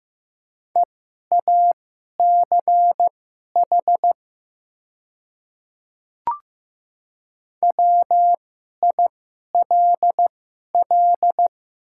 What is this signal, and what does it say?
Morse code, heard as a keyed tone, about 15 wpm, spaced as for 11 wpm, 700 Hz. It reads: EACH WILL
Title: 2x Character Spacing - Top 300 Words - 15wpm